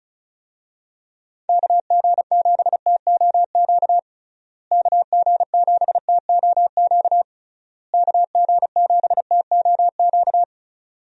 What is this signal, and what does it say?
Morse code, heard as a keyed tone, about 34 wpm, 700 Hz. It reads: KG7TOQ KG7TOQ KG7TOQ
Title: USA Call Signs - 6 Characters - Repeated 3x - 35wpm